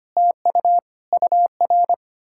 Morse code, heard as a keyed tone, about 33 wpm, 700 Hz. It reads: TU UR